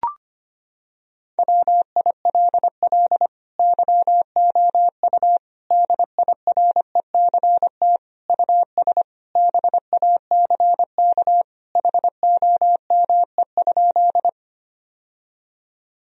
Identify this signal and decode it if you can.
Morse code, heard as a keyed tone, about 25 wpm, 700 Hz. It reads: WILL YOU DIRECT US BACK HOME?
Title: Sentences from Top 1000 Words - Review Part 2 - 25wpm